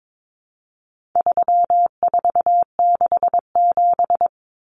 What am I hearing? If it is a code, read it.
Morse code, heard as a keyed tone, about 22 wpm, 700 Hz. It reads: 3467
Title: Numbers - 4-Digit 22wpm